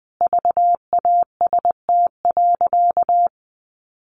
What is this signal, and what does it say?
Morse code, heard as a keyed tone, about 20 wpm, 700 Hz. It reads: VAST.
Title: Sentences from Top 1300 Words 20wpm